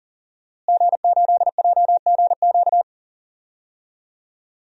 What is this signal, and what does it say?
Morse code, heard as a keyed tone, about 40 wpm, 700 Hz. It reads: G8JGQ
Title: DXCC-100 Call Signs - 5 Characters - Repeated 2x - 40wpm